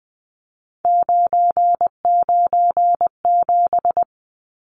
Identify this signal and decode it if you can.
Morse code, heard as a keyed tone, about 20 wpm, 700 Hz. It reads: 997
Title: Numbers - 3-Digit 20wpm